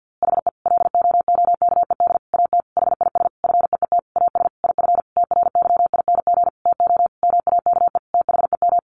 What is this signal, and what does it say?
Morse code, heard as a keyed tone, about 50 wpm, 700 Hz. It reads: HE LOOKED AT HIS FEET AS IF TRYING TO MAKE THEM
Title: A Sound Of Thunder 50wpm